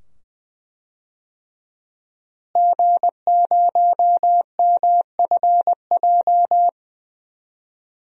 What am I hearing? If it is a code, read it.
Morse code, heard as a keyed tone, about 21 wpm, 700 Hz. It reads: G0MFJ
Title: DXCC-100 Call Signs - 5 Characters - Repeated 2x - 20wpm